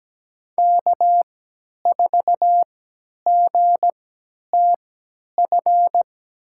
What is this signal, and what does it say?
Morse code, heard as a keyed tone, about 17 wpm, spaced as for 10 wpm, 700 Hz. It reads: K4GTF
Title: USA Call Signs - 3x Character Spacing - 17wpm